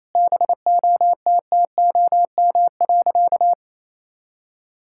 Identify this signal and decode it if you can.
Morse code, heard as a keyed tone, about 28 wpm, 700 Hz. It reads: BOTTOM.
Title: Sentences from Top 1000 Words - Review Part 6 - 28wpm